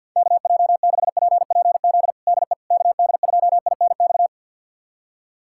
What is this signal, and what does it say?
Morse code, heard as a keyed tone, about 53 wpm, 700 Hz. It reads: KJ6PPZ DE KD2INX
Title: USA Callsign DE Callsign 50wpm